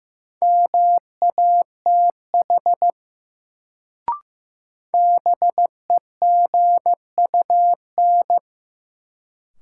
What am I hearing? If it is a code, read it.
Morse code, heard as a keyed tone, about 15 wpm, 700 Hz. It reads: MATH BEGUN
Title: Sentences from Top 1000 Words - Review Part 6 - New Word Drill - 15wpm